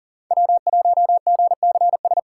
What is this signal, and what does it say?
Morse code, heard as a keyed tone, about 40 wpm, 700 Hz. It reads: W1GCS
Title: K1USN SST QSO Practice - 40wpm